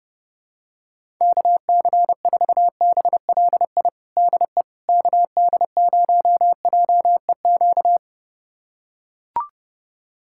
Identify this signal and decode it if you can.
Morse code, heard as a keyed tone, about 30 wpm, 700 Hz. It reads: KC4BLI DE KD0JEQ
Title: USA Callsign DE Callsign 30wpm